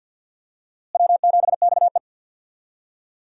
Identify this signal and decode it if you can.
Morse code, heard as a keyed tone, about 50 wpm, 700 Hz. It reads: W7XE
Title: USA Call Signs - 4 Characters - Repeated 3x - 50wpm